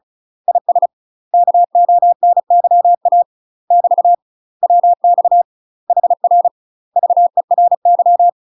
Morse code, heard as a keyed tone, about 34 wpm, 700 Hz. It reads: IS KONYA <BT> WX HR VERY
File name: UK MASTER QSOs TRAINING / 35WPM / UK MASTER QSOs 0015